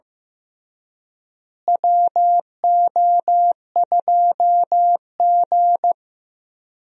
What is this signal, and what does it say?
Morse code, heard as a keyed tone, about 15 wpm, 700 Hz. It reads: WO2G
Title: USA Call Signs - 4 Characters - Repeated 2x - 15wpm